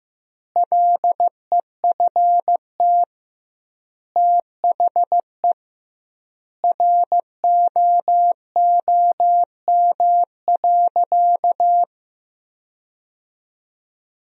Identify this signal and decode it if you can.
Morse code, heard as a keyed tone, about 15 wpm, 700 Hz. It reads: LEFT THE ROOM.
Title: 2x Word-Spacing - Sentences from Top 300 Words - 15wpm